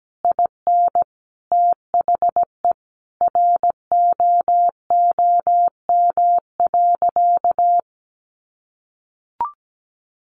Morse code, heard as a keyed tone, about 17 wpm, 700 Hz. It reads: IN THE ROOM.
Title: A Sound Of Thunder 17wpm